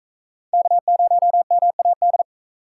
Morse code, heard as a keyed tone, about 41 wpm, 700 Hz. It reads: K0MAD